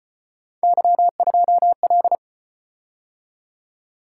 Morse code, heard as a keyed tone, about 35 wpm, 700 Hz. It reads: Y2L